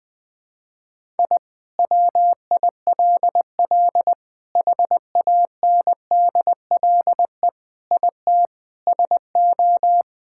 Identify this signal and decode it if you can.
Morse code, heard as a keyed tone, about 20 wpm, 700 Hz. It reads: I WILL HANDLE IT SO
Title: Sentences with 7-Word Phrases - 20wpm